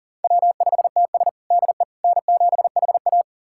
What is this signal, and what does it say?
Morse code, heard as a keyed tone, about 40 wpm, 700 Hz. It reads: W5TS DE N7HA
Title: USA Callsign DE Callsign - 4 Characters - 40wpm